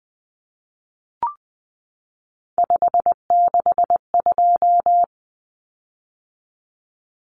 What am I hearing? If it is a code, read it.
Morse code, heard as a keyed tone, about 20 wpm, 700 Hz. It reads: 562
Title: Common Cut Numbers - 3-Digit 20wpm